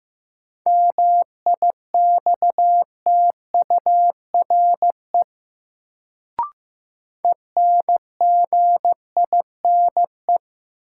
Morse code, heard as a keyed tone, about 15 wpm, 700 Hz. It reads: MIXTURE ENGINE
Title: Sentences from Top 1500 Words - New Word Drill - 15wpm